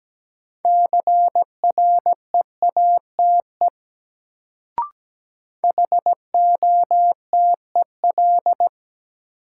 Morse code, heard as a keyed tone, about 17 wpm, 700 Hz. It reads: CREATE HOTEL
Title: Sentences from Top 1600 Words - New Word Drill Alt - 17wpm